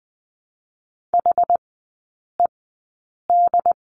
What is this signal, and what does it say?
Morse code, heard as a keyed tone, about 20 wpm, spaced as for 8 wpm, 700 Hz. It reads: HED
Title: 067 Lesson 12e - Letter d - Three Characters 20wpm